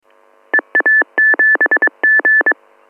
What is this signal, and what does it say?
Morse code, heard as a keyed tone, about 22 wpm, 1750 Hz. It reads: EA7G